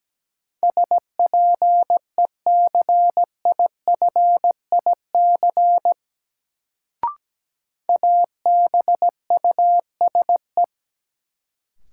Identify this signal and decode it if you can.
Morse code, heard as a keyed tone, about 17 wpm, 700 Hz. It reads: SPECIFIC ABUSE
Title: Sentences from Top 1300 Words - New Word Drill Alt - 17wpm